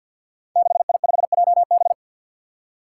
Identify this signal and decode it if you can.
Morse code, heard as a keyed tone, about 53 wpm, 700 Hz. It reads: 6I5JB
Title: Five Challenging Letter-Number Combinations 50wpm